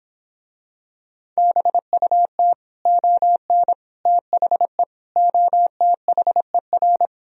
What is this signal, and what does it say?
Morse code, heard as a keyed tone, about 26 wpm, 700 Hz. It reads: BUT ON THE OTHER